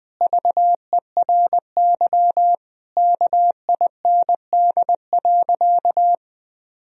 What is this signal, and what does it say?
Morse code, heard as a keyed tone, about 20 wpm, 700 Hz. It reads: VERY KIND.
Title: Sentences from Top 1000 Words - Review Part 3 - 20wpm